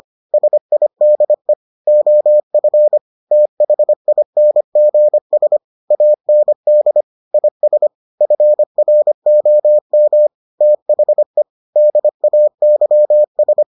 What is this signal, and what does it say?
Morse code, heard as a keyed tone, about 25 wpm, 600 Hz. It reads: SIDE OF THINGS AND IS FROM THE DAYS